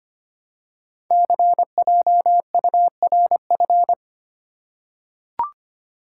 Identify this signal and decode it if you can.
Morse code, heard as a keyed tone, about 26 wpm, 700 Hz. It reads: CJURF
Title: Code Groups - 5 Letters - 25wpm